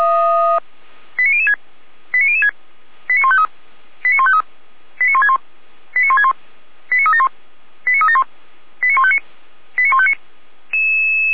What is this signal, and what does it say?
Five-tone selective call (ZVEI), heard as quick runs of five tones, tones of about 70 ms. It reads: 89006 89006 89153 89153 89161 89161 89371 89371 89159 89159